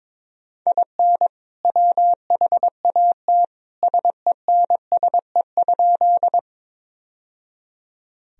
Morse code, heard as a keyed tone, about 22 wpm, 700 Hz. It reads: IN WHAT SENSE?
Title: Sentences from Top 1000 Words - Review Part 4 - 22wpm